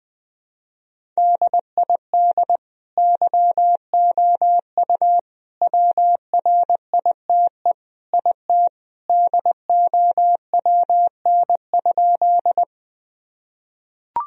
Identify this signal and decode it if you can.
Morse code, heard as a keyed tone, about 21 wpm, 700 Hz. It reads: DID YOU WRITE IT DOWN?
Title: Sentences from Top 1000 Words - Review Part 1 - 20wpm